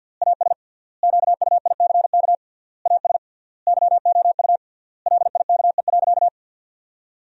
Morse code, heard as a keyed tone, about 52 wpm, 700 Hz. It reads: AS QUICK AS YOU LIKE.
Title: 3x Word-Spacing - Sentences from Top 500 Words - 50wpm